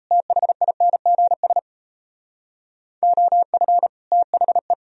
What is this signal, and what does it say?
Morse code, heard as a keyed tone, about 36 wpm, 700 Hz. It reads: THINGS OF THE